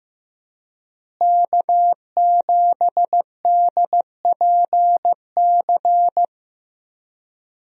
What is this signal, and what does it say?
Morse code, heard as a keyed tone, about 15 wpm, 700 Hz. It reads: K7DPC
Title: USA Call Signs - 5 Characters - Repeated 2x - 15wpm